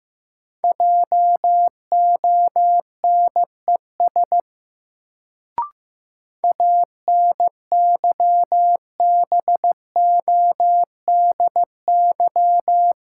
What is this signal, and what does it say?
Morse code, heard as a keyed tone, about 15 wpm, 700 Hz. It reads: JONES ANYBODY
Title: Sets of 5 Words - New Word Drill - 15wpm